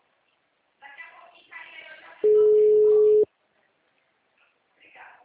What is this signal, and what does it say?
Ringback tone (Europe, 425 Hz), with speech.